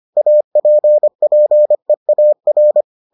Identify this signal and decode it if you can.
Morse code, heard as a keyed tone, about 25 wpm, 600 Hz. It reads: APPEAR